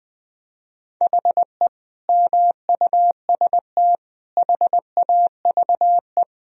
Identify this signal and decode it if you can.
Morse code, heard as a keyed tone, about 20 wpm, 700 Hz. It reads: HE MUST HAVE
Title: Sentences from Top 1000 Words - Review Part 4 - 20wpm